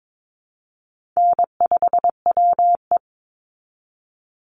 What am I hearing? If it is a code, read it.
Morse code, heard as a keyed tone, about 22 wpm, 690 Hz. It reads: N5WE
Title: Easy Call Signs Repeated 2x - 22wpm